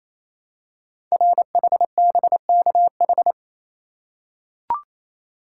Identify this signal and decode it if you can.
Morse code, heard as a keyed tone, about 28 wpm, 700 Hz. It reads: RHBKH
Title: Code Groups - 5 Letters - 28wpm